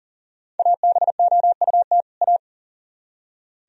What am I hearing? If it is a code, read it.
Morse code, heard as a keyed tone, about 52 wpm, 700 Hz. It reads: ABOUT A